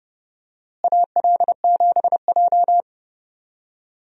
Morse code, heard as a keyed tone, about 30 wpm, 700 Hz. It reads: AL7J
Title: USA Call Signs - 4 Characters - Repeated 2x - 30wpm